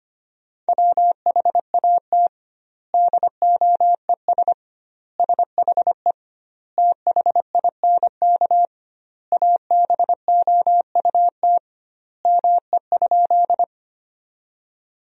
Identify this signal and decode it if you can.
Morse code, heard as a keyed tone, about 25 wpm, 700 Hz. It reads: WHAT DOES SHE THINK ABOUT ME?
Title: 2x Word-Spacing - Sentences from Top 200 Words - 25wpm